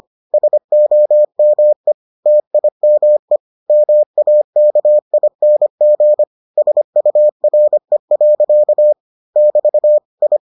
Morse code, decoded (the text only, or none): SOME TIME MAKING SURE. <BT> I